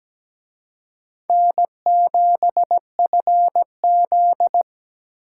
Morse code, heard as a keyed tone, about 20 wpm, 700 Hz. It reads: N7FZ
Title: Speed Racing - USA 1x2 Call Signs 17wpm